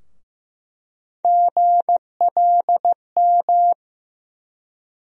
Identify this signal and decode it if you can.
Morse code, heard as a keyed tone, about 15 wpm, 700 Hz. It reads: GLM
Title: USA Call Sign Postfixes - 3 Letters 15wpm